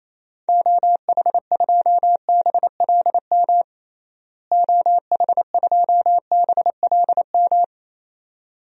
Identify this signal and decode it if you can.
Morse code, heard as a keyed tone, about 27 wpm, 700 Hz. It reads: OH2BLM OH2BLM
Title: DXCC-100 Call Signs - 6 Characters - Repeated 2x - 28wpm